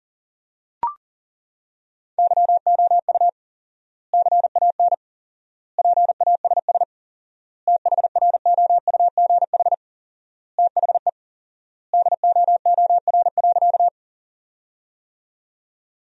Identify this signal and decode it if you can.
Morse code, heard as a keyed tone, about 40 wpm, 700 Hz. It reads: YOU CAN PASS THROUGH THE DOOR.
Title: 4x Word-Spacing - Sentences from Top 400 Words - 40wpm